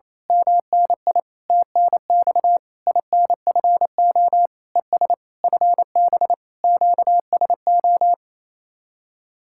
Morse code, heard as a keyed tone, about 26 wpm, 700 Hz. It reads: MNI TNX INFO ES FB QSO
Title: Speed-Racing Common QSO Phrases Encore - 28wpm